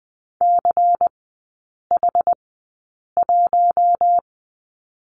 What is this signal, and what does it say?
Morse code, heard as a keyed tone, about 20 wpm, spaced as for 8 wpm, 700 Hz. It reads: CH1
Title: 093 Lesson 16e - Letter c - Three Characters 20wpm